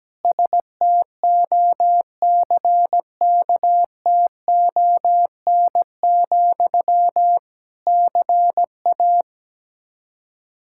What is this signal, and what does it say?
Morse code, heard as a keyed tone, about 17 wpm, 700 Hz. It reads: STOCKTON, CA
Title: Practice QSO - Exchange 1 - Beginning - 17wpm